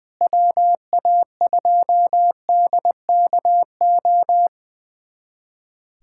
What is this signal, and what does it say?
Morse code, heard as a keyed tone, about 21 wpm, 700 Hz. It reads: WA2DKO
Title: USA Callsign DE Callsign - 6 Characters - 20wpm